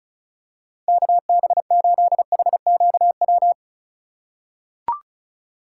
Morse code, heard as a keyed tone, about 34 wpm, 700 Hz. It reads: KB8HQW
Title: USA Call Signs - 6 Characters - 2x Character Spacing - 35wpm